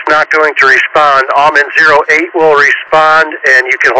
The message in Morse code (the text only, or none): none